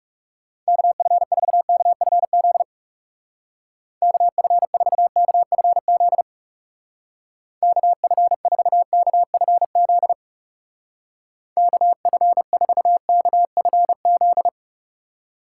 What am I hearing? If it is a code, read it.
Morse code, heard as a keyed tone, about 36 wpm, 700 Hz. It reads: KF4KFZ KF4KFZ KF4KFZ KF4KFZ
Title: Speed Racing - USA 2x3 Call Signs 30wpm